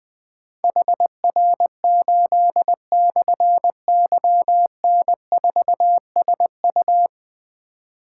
Morse code, heard as a keyed tone, about 20 wpm, 700 Hz. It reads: HR8/YN4SU
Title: Difficult Call Signs 20wpm